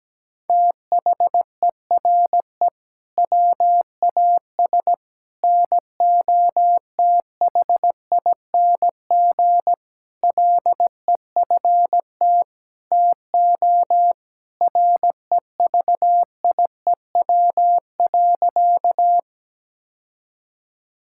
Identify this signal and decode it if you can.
Morse code, heard as a keyed tone, about 17 wpm, 700 Hz. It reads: THERE WAS NOTHING LEFT TO REVIEW.
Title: Sentences from Top 1000 Words - Review Part 6 - 17wpm